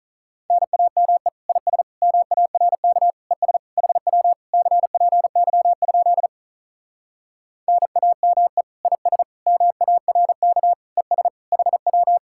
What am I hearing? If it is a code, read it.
Morse code, heard as a keyed tone, about 44 wpm, 700 Hz. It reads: NAME IS MARK ES HW CPY? NAME IS MARK ES HW